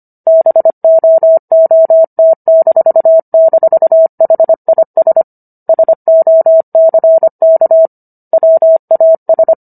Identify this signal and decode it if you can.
Morse code, heard as a keyed tone, about 25 wpm, 650 Hz. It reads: BOOT--HIS SOCK WAS